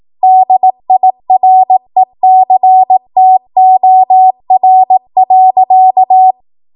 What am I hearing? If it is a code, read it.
Morse code, heard as a keyed tone, about 18 wpm, 750 Hz. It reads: DIRECTOR.